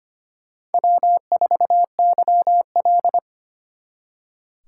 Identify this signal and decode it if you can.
Morse code, heard as a keyed tone, about 32 wpm, 700 Hz. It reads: W4YL